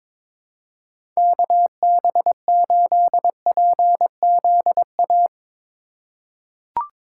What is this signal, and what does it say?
Morse code, heard as a keyed tone, about 22 wpm, 700 Hz. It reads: KB8PZA
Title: USA Call Signs - 6 Characters - 3x Character Spacing - 22wpm